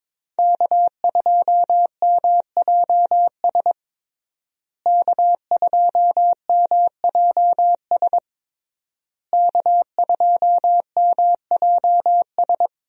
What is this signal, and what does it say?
Morse code, heard as a keyed tone, about 22 wpm, 700 Hz. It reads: K2MJS K2MJS K2MJS
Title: USA Call Signs - 5 Characters - Repeated 3x - 22wpm